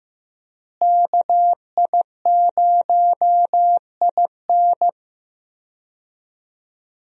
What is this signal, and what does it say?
Morse code, heard as a keyed tone, about 15 wpm, 700 Hz. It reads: KI0IN
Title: USA Call Signs - 5 Characters - Repeated 2x - 15wpm